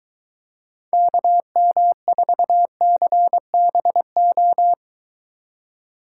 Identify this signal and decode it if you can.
Morse code, heard as a keyed tone, about 24 wpm, 700 Hz. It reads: KM4CBO